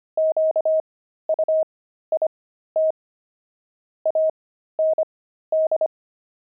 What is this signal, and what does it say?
Morse code, heard as a keyed tone, about 25 wpm, spaced as for 13 wpm, 630 Hz. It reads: QUIT AND
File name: pangram-fs-13-cs-25-quit